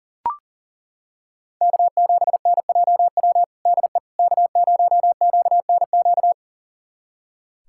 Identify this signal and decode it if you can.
Morse code, heard as a keyed tone, about 40 wpm, 700 Hz. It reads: K7NJW DE K0QNQ
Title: USA Callsign DE Callsign - 5 Characters - 40wpm